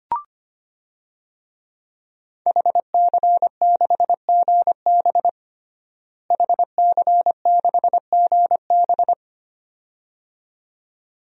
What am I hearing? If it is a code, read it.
Morse code, heard as a keyed tone, about 25 wpm, 700 Hz. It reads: HC6GB HC6GB
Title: DXCC-100 Call Signs - 5 Characters - Repeated 2x - 25wpm